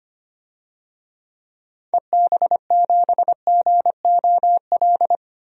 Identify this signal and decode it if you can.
Morse code, heard as a keyed tone, about 25 wpm, 700 Hz. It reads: EB7GOL
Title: DXCC-100 Call Signs - 6 Characters - 25wpm